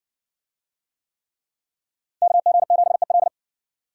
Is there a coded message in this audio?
Yes: Morse code from 2.2 to 3.3 s, heard as a keyed tone, about 60 wpm, 700 Hz. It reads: KG7EL